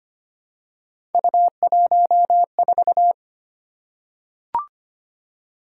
Morse code, heard as a keyed tone, about 25 wpm, 700 Hz. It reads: U14
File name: Three Letter-Number Combinations - 25wpm